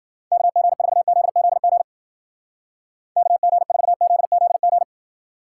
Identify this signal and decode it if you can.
Morse code, heard as a keyed tone, about 52 wpm, 700 Hz. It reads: KG4ZZG KG4ZZG